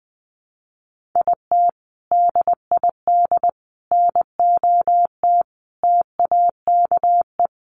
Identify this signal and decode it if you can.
Morse code, heard as a keyed tone, about 19 wpm, 700 Hz. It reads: IT DID NOT TAKE
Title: Sentences from Top 600 Words 20wpm